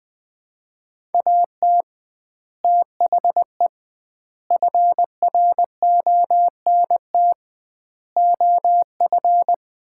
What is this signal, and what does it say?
Morse code, heard as a keyed tone, about 21 wpm, 700 Hz. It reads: AT THE FRONT OF
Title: 2x Word-Spacing - Sets of 5 Words - 20wpm